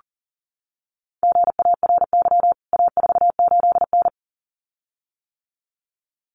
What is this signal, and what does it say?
Morse code, heard as a keyed tone, about 40 wpm, 700 Hz. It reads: GARY A48N
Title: CWT Exchange 40wpm